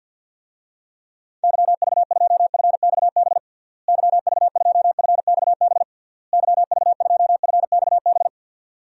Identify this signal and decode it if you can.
Morse code, heard as a keyed tone, about 52 wpm, 700 Hz. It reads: YV2FXB YV2FXB YV2FXB
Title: DXCC-100 Call Signs - 6 Characters - Repeated 3x - 50wpm